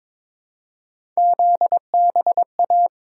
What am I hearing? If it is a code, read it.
Morse code, heard as a keyed tone, about 22 wpm, 700 Hz. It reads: ZBA
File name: practice / 22wpm / Three Letter-Number Combinations - 22wpm